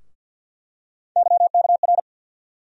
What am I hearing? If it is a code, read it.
Morse code, heard as a keyed tone, about 48 wpm, 700 Hz. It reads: YKR
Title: USA Call Sign Postfixes - 3 Letters 50wpm